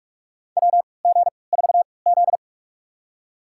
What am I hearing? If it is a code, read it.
Morse code, heard as a keyed tone, about 48 wpm, spaced as for 26 wpm, 700 Hz. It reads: WG4Z
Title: USA Call Signs - 3x Character Spacing - 45wpm